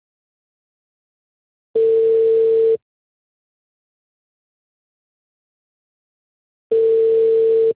Ringback tone (China, 450 Hz).